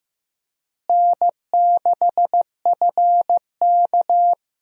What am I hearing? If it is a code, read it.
Morse code, heard as a keyed tone, about 15 wpm, 700 Hz. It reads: N6FK
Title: USA Call Signs - 4 Characters - Repeated 3x - 15wpm